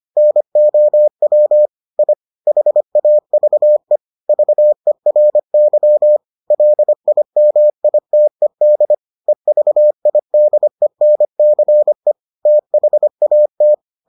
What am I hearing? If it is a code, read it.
Morse code, heard as a keyed tone, about 25 wpm, 600 Hz. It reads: NOW I HAVE VERY LIMITED EVIDENCE THAT